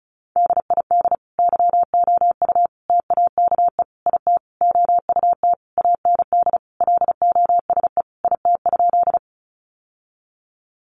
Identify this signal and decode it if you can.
Morse code, heard as a keyed tone, about 34 wpm, 700 Hz. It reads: DID YOU TAKE IT OUT AND LOSE IT?
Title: Sentences from Top 1100 Words 35wpm